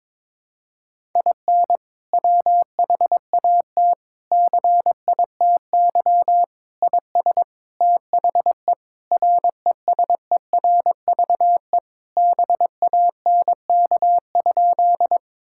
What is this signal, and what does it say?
Morse code, heard as a keyed tone, about 22 wpm, 700 Hz. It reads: IN WHAT CITY IS THE RESERVE BANK?
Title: Sentences from Top 1000 Words - Review Part 6 - 22wpm